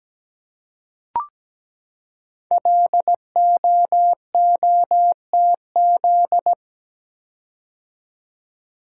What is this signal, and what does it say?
Morse code, heard as a keyed tone, about 17 wpm, 700 Hz. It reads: LOOTZ
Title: Code Groups - 5 Letters - 17wpm